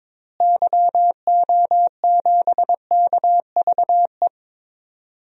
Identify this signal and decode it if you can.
Morse code, heard as a keyed tone, about 22 wpm, 700 Hz. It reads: YO7KVE
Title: DXCC-100 Call Signs - 6 Characters - Repeated 3x - 22wpm